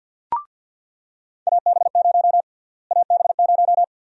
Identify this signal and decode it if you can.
Morse code, heard as a keyed tone, about 48 wpm, 700 Hz. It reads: AB0 AB0
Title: USA Call Sign Prefixes - Repeated 2x - 50wpm